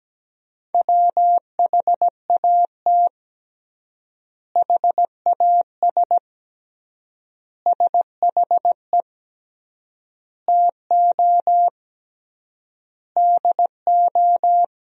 Morse code, heard as a keyed tone, about 17 wpm, 700 Hz. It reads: WHAT HAS SHE TO DO
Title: 3x Word-Spacing - Sentences from Top 100 Words - 17wpm